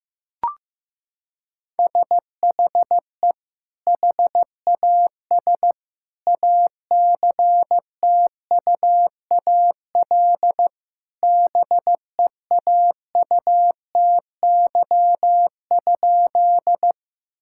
Morse code, heard as a keyed tone, about 15 wpm, 700 Hz. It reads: SHE HAS ACTUAL BEAUTY?
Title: Sentences from Top 1000 Words - Review Part 5 - 15wpm